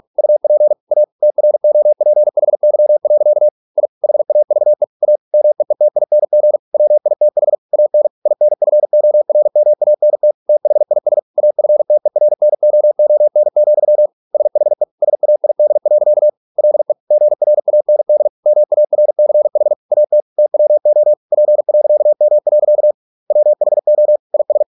Morse code, heard as a keyed tone, about 46 wpm, 600 Hz. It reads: UP A TROPHY. I HAVE A MEETING WITH AN INFORMANT THIS AFTERNOON, SHE SAID. LE GRAND MARCH AT TWO P.M. WHO IS